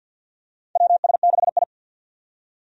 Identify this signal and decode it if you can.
Morse code, heard as a keyed tone, about 51 wpm, 700 Hz. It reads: WS6I
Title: USA Call Signs - 4 Characters - 50wpm